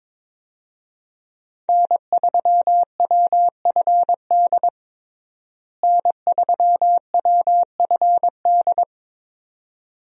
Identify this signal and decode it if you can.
Morse code, heard as a keyed tone, about 22 wpm, 700 Hz. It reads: N3WFD N3WFD
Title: USA Call Signs - 5 Characters - Repeated 2x - 22wpm